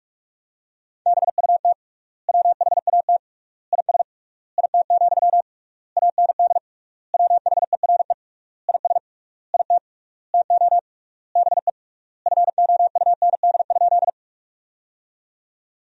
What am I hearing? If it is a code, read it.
Morse code, heard as a keyed tone, about 46 wpm, 700 Hz. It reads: BUT WHAT IS IT, AND WHERE IS IT TO BE FOUND?
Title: 3x Word-Spacing - Sentences from Top 300 Words - 45wpm